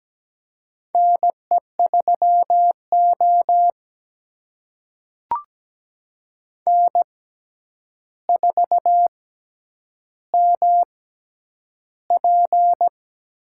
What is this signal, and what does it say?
Morse code, heard as a keyed tone, about 17 wpm, spaced as for 9 wpm, 700 Hz. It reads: NE3O N4MP
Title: USA Call Signs - 4 Characters - 6x Character Spacing - 17wpm